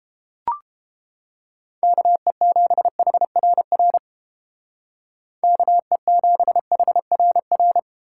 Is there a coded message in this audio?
Yes: Morse code from 1.8 to 7.8 s, heard as a keyed tone, about 28 wpm, 700 Hz. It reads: KE7HRR KE7HRR